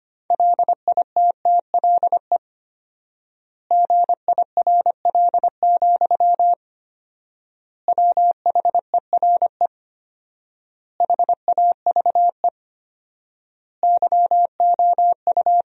Morse code, heard as a keyed tone, about 25 wpm, 700 Hz. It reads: LITTLE GIRL, WHERE HAVE YOU
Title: 4x Word-Spacing - Sentences from Top 400 Words - 25wpm